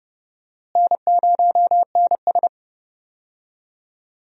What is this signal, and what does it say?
Morse code, heard as a keyed tone, about 31 wpm, 700 Hz. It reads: N0NS
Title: USA Call Signs - 4 Characters - Repeated 2x - 30wpm